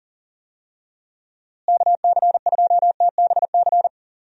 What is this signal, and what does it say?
Morse code, heard as a keyed tone, about 40 wpm, 700 Hz. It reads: KC2TBC
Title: USA Call Signs - 6 Characters - Repeated 3x - 40wpm